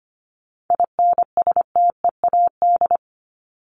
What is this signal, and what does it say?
Morse code, heard as a keyed tone, about 25 wpm, 700 Hz. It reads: INSTEAD
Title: Top 501-750 Words 25wpm